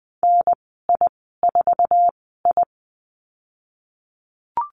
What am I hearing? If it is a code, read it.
Morse code, heard as a keyed tone, about 20 wpm, spaced as for 14 wpm, 700 Hz. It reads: NI4I
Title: 062 Lesson 11g - Letter h - Call Signs 20wpm